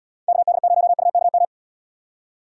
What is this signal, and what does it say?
Morse code, heard as a keyed tone, about 75 wpm, 700 Hz. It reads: KD0UKM